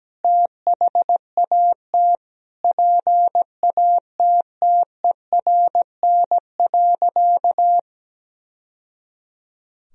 Morse code, heard as a keyed tone, about 17 wpm, 700 Hz. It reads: THAT PATTERN.
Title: Sentences from Top 1000 Words - Review Part 2 - 17wpm